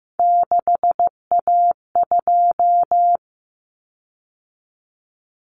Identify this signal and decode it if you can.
Morse code, heard as a keyed tone, about 15 wpm, 700 Hz. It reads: 6A2
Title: Common Cut Numbers - 3-Digit 15wpm